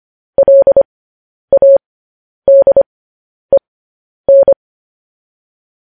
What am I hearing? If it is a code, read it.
Morse code, heard as a keyed tone, about 25 wpm, spaced as for 10 wpm, 550 Hz. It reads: LADEN